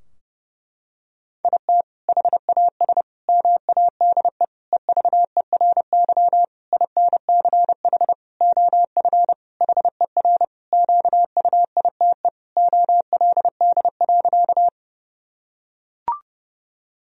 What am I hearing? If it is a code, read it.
Morse code, heard as a keyed tone, about 32 wpm, 700 Hz. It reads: IT HAS MADE EVERY INCH OF HER QUITE OLD.
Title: Sentences from Top 1000 Words - Review Part 3 - 30wpm